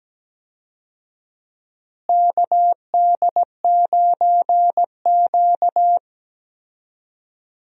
Morse code, heard as a keyed tone, about 17 wpm, 700 Hz. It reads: KD9Q
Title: USA Call Signs - 4 Characters - Repeated 3x - 17wpm